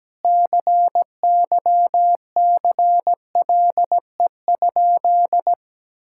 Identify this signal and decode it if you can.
Morse code, heard as a keyed tone, about 17 wpm, 700 Hz. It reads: CYCLE?
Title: Sentences from Top 1000 Words - Review Part 6 - 17wpm